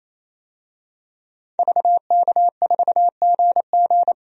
Morse code, heard as a keyed tone, about 28 wpm, 700 Hz. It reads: VK4GG
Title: DXCC-100 Call Signs - 5 Characters - 28wpm